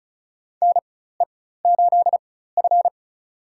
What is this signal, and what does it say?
Morse code, heard as a keyed tone, about 34 wpm, spaced as for 16 wpm, 700 Hz. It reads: NE8F
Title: USA Call Signs - 4 Characters - 4x Character Spacing - 35wpm